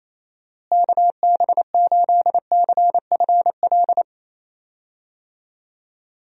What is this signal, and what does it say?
Morse code, heard as a keyed tone, about 28 wpm, 700 Hz. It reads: KB8CFL
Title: USA Call Signs - 6 Characters - Repeated 2x - 28wpm